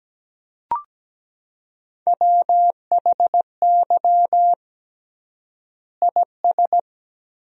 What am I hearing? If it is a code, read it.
Morse code, heard as a keyed tone, about 17 wpm, 700 Hz. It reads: WHY IS
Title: 3x Word-Spacing - Sentences from Top 200 Words - 17wpm